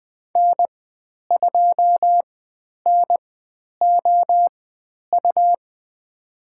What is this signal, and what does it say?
Morse code, heard as a keyed tone, about 20 wpm, spaced as for 10 wpm, 700 Hz. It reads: N2NOU
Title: 109 Lesson 18g - Letter m - Call Signs 20wpm